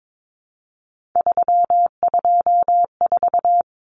Morse code, heard as a keyed tone, about 22 wpm, 700 Hz. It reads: 324
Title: Numbers - 3-Digit 22wpm